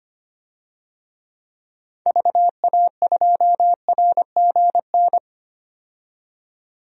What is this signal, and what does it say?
Morse code, heard as a keyed tone, about 25 wpm, 700 Hz. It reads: VA2RGN
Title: DXCC-100 Call Signs - 6 Characters - 25wpm